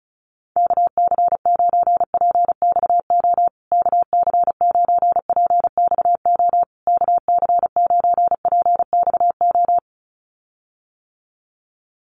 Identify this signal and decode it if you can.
Morse code, heard as a keyed tone, about 34 wpm, 700 Hz. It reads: KC9PXO KC9PXO KC9PXO
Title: USA Call Signs Repeated 3x 35wpm